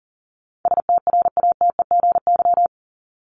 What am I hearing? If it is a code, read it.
Morse code, heard as a keyed tone, about 40 wpm, 700 Hz. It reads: STRATEGY
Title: Top 501-750 Words 40wpm